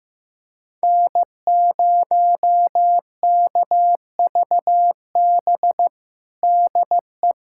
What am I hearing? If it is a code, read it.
Morse code, heard as a keyed tone, about 15 wpm, 700 Hz. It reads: N0KVB DE
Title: USA Callsign DE Callsign - 5 Characters - 15wpm